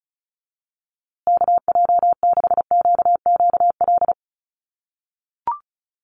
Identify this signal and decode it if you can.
Morse code, heard as a keyed tone, about 34 wpm, 700 Hz. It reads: KJ6QQL
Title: USA Call Signs Repeated 3x 35wpm